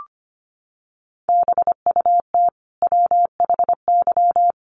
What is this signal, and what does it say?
Morse code, heard as a keyed tone, about 25 wpm, 700 Hz. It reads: BUT WHY
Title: Sentences from Top 1100 Words 25wpm